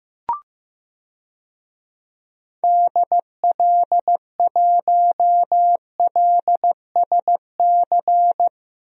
Morse code, heard as a keyed tone, about 15 wpm, 700 Hz. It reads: DL1LSC
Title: DXCC-100 Call Signs - 6 Characters - Repeated 2x - 15wpm